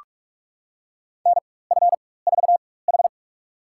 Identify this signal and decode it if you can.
Morse code, heard as a keyed tone, about 42 wpm, spaced as for 20 wpm, 700 Hz. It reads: NF4H
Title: USA Call Signs - 4 Characters - 4x Character Spacing - 45wpm